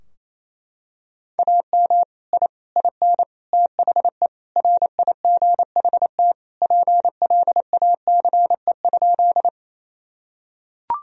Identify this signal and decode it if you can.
Morse code, heard as a keyed tone, about 28 wpm, 700 Hz. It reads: AM I IN THE RIGHT PLACE?
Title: Sentences from Top 1000 Words - Review Part 2 - 28wpm